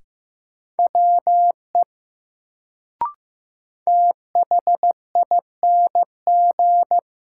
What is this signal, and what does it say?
Morse code, heard as a keyed tone, about 15 wpm, 700 Hz. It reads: WE THING
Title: Sentences from Top 100 Words - New Word Drill - 15wpm